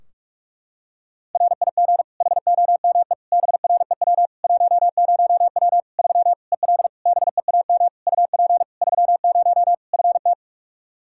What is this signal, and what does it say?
Morse code, heard as a keyed tone, about 41 wpm, 700 Hz. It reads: RIG HOME BREW 10W 3 EL BEAM UP 30 FT
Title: Practice QSO - Exchange 2 - Info - 45wpm